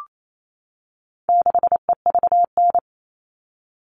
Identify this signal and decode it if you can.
Morse code, heard as a keyed tone, about 28 wpm, 700 Hz. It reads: 6EVN
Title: All Cut Numbers - 4-Digit 28wpm